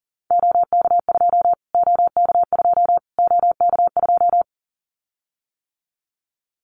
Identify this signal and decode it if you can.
Morse code, heard as a keyed tone, about 40 wpm, 700 Hz. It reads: OK2 OK2 OK2
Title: Callsign Prefixes for Top 25 DXCC - Repeated 3x - 40wpm